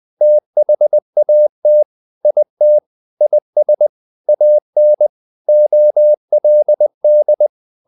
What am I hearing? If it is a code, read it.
Morse code, heard as a keyed tone, about 20 wpm, 600 Hz. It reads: THAT IT IS AN OLD
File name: the old man and the sea / the old man and the sea part 1 20wpm